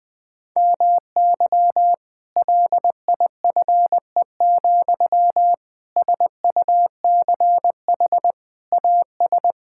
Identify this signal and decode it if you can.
Morse code, heard as a keyed tone, about 20 wpm, 700 Hz. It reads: MY LIFE, SUCH AS
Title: Sentences from Top 1600 Words 20wpm